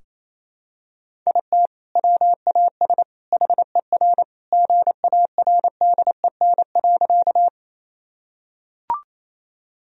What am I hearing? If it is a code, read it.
Morse code, heard as a keyed tone, about 28 wpm, 700 Hz. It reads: IT WAS HER GARDEN.
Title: Sentences from Top 1000 Words - Review Part 4 - 28wpm